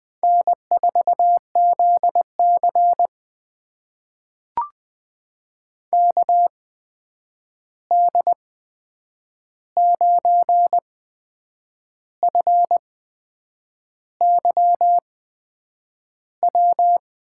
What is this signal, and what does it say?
Morse code, heard as a keyed tone, about 20 wpm, spaced as for 8 wpm, 700 Hz. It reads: N4ZC KD9FYW
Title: USA Call Signs - 8x Character Spacing - 20wpm